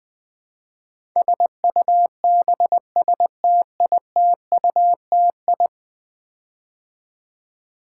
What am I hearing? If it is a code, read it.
Morse code, heard as a keyed tone, about 21 wpm, 700 Hz. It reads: SUBSTITUTI
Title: Build A Word - 5 Letters At A Time - 12-Letter Words - 20wpm